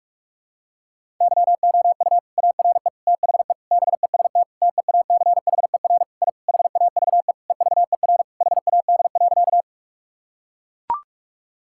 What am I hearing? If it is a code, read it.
Morse code, heard as a keyed tone, about 41 wpm, 700 Hz. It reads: YOU ARE THE BEST TEACHER I HAVE EVER HAD.
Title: Sentences from Top 1000 Words - Review Part 6 - 45wpm